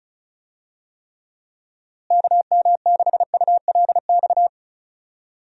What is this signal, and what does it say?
Morse code, heard as a keyed tone, about 34 wpm, 700 Hz. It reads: KM6ULX
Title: USA Call Signs - 6 Characters - Repeated 2x - 35wpm